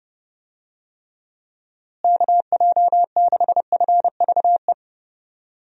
Morse code, heard as a keyed tone, about 30 wpm, 700 Hz. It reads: KJ6FVE
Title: USA Call Signs - 6 Characters - Repeated 3x - 30wpm